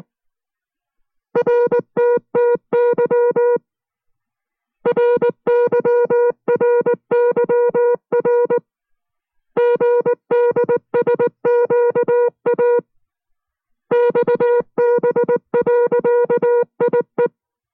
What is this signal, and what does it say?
Morse code, heard as a keyed tone, about 19 wpm, 480 Hz. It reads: RTTY RYRYR GDSQA XB.IE